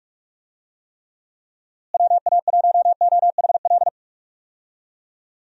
Morse code, heard as a keyed tone, about 46 wpm, 700 Hz. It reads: WA1OHL
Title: USA Call Signs - 6 Characters - Repeated 2x - 45wpm